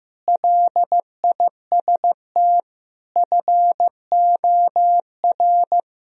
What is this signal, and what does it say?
Morse code, heard as a keyed tone, about 15 wpm, 700 Hz. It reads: LIST FOR
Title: Sentences from Top 1000 Words - Review Part 2 - 15wpm